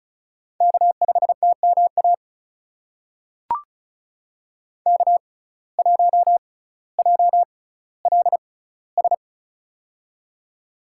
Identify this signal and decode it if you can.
Morse code, heard as a keyed tone, about 36 wpm, spaced as for 18 wpm, 700 Hz. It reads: K5TMA K1JLS
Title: USA Call Signs - 5 Characters - 6x Character Spacing - 35wpm